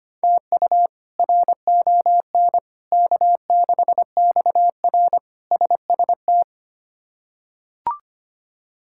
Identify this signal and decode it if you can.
Morse code, heard as a keyed tone, about 25 wpm, 700 Hz. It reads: TU RON K6XR SST
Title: K1USN SST QSO Practice - 25wpm